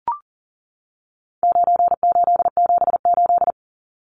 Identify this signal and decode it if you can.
Morse code, heard as a keyed tone, about 40 wpm, 700 Hz. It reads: 9878
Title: Numbers - 4-Digit 40wpm